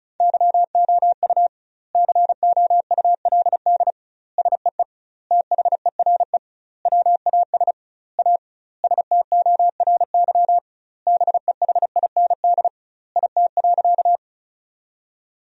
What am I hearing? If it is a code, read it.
Morse code, heard as a keyed tone, about 34 wpm, 700 Hz. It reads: YOU COULD SEE THERE WAS A STORY BEHIND IT.
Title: 2x Word-Spacing - Sentences from Top 500 Words - 35wpm